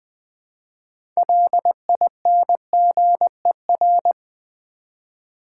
Morse code, heard as a keyed tone, about 20 wpm, 700 Hz. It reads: LINGER
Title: 6 Letter Words - First 3 Letters Given - 20wpm